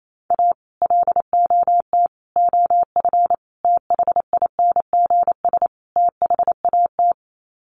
Morse code, heard as a keyed tone, about 28 wpm, 700 Hz. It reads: A LOT OF THINGS THAT
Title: Sets of 5 Words 28wpm